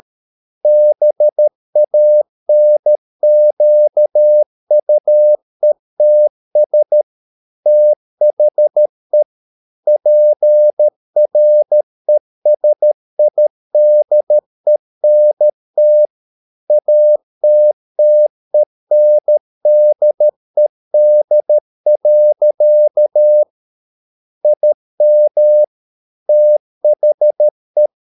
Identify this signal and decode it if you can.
Morse code, heard as a keyed tone, about 13 wpm, 600 Hz. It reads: BANQUETS THE PRESIDENT ATTENDED. IM THE